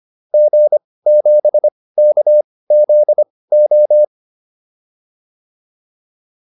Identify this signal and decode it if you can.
Morse code, heard as a keyed tone, about 25 wpm, spaced as for 18 wpm, 600 Hz. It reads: G7KZO